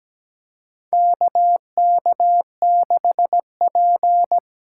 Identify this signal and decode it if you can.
Morse code, heard as a keyed tone, about 17 wpm, 700 Hz. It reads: KK6P
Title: USA Call Signs - 4 Characters - 8x Character Spacing - 17wpm